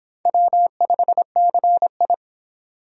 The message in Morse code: W5CI

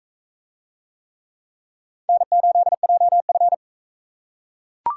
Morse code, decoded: N8JF

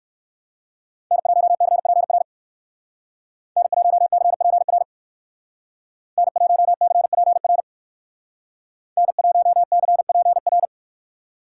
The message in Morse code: N1CPR N1CPR N1CPR N1CPR

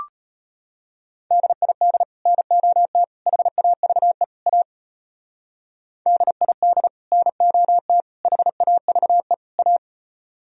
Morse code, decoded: DID NOT HAVE A DID NOT HAVE A

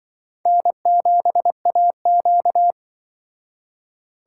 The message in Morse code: N7AQ

